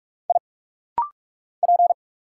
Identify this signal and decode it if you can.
Morse code, heard as a keyed tone, about 46 wpm, 690 Hz. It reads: I P
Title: Single Letters - Rapid-Fire 45wpm